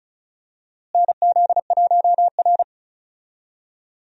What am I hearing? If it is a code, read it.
Morse code, heard as a keyed tone, about 36 wpm, 700 Hz. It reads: NZ1R
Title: USA Call Signs - 4 Characters - Repeated 3x - 35wpm